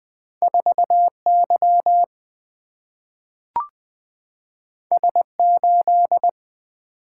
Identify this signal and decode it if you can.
Morse code, heard as a keyed tone, about 20 wpm, 700 Hz. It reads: 4 Y S 8
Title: Single-Letter Single-Number 20wpm